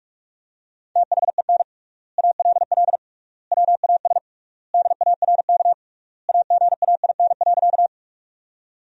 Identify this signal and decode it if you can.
Morse code, heard as a keyed tone, about 46 wpm, 700 Hz. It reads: THEN ALL WAS DARK AGAIN.
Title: 3x Word-Spacing - Sentences from Top 500 Words - 45wpm